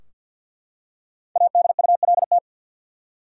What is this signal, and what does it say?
Morse code, heard as a keyed tone, about 49 wpm, 700 Hz. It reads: ADULT
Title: Five Letter Words - 50wpm